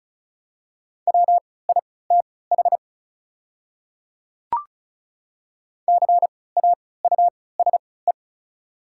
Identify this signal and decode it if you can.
Morse code, heard as a keyed tone, about 36 wpm, spaced as for 20 wpm, 700 Hz. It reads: WITH CAUSE
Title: 3x Character Spacing - Top 200 Words - 35wpm